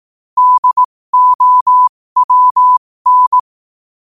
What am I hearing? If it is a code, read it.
Morse code, heard as a keyed tone, about 18 wpm, spaced as for 15 wpm, 1000 Hz. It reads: DOWN
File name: qotdfiles / 15wpm